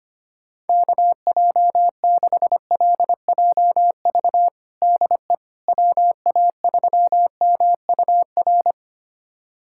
Morse code, heard as a keyed tone, about 25 wpm, 700 Hz. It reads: KJ6LJV DE WA3MUR
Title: USA Callsign DE Callsign - 6 Characters - 25wpm